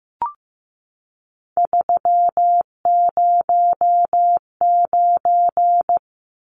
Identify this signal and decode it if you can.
Morse code, heard as a keyed tone, about 15 wpm, 700 Hz. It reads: 309
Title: Numbers - 3-Digit 15wpm